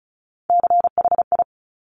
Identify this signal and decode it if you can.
Morse code, heard as a keyed tone, about 36 wpm, 690 Hz. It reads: CHI
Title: Top Three Letter Combinations in Words 35wpm